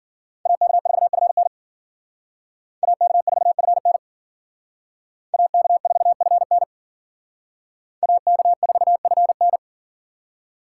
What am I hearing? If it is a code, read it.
Morse code, heard as a keyed tone, about 51 wpm, 700 Hz. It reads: AK4FN AK4FN AK4FN AK4FN